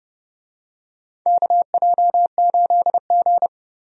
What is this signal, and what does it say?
Morse code, heard as a keyed tone, about 30 wpm, 700 Hz. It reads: KJ8G
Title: USA Call Signs - 4 Characters - 30wpm